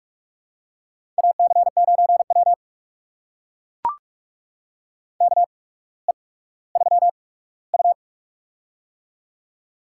Morse code, heard as a keyed tone, about 44 wpm, spaced as for 19 wpm, 700 Hz. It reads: AC9W KE3U
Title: USA Call Signs - 4 Characters - 8x Character Spacing - 45wpm